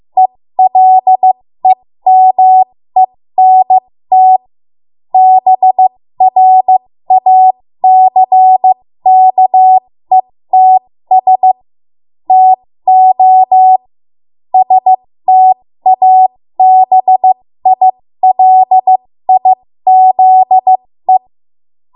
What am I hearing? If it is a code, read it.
Morse code, heard as a keyed tone, about 15 wpm, spaced as for 13 wpm, 750 Hz. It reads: ELEMENT BRACKETS TO STABILIZE
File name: Morse / 13WPM